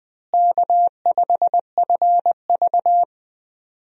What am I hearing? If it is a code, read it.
Morse code, heard as a keyed tone, about 21 wpm, 700 Hz. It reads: K5FV
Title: USA Call Signs - 4 Characters - Repeated 3x - 20wpm